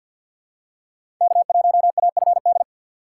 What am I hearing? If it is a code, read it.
Morse code, heard as a keyed tone, about 49 wpm, 700 Hz. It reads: K1AFD
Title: USA Call Signs - 4x Character Spacing - 50wpm